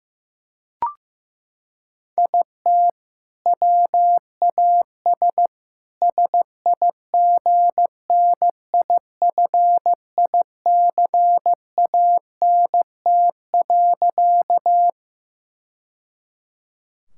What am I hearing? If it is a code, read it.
Morse code, heard as a keyed tone, about 16 wpm, 700 Hz. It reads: IT WAS SIGNIFICANT.
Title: Sentences from Top 1000 Words - Review Part 6 - 15wpm